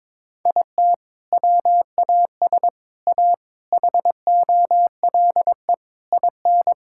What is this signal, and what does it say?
Morse code, heard as a keyed tone, about 22 wpm, 700 Hz. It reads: IT WAS A HOLE IN
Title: Sentences from Top 1600 Words 22wpm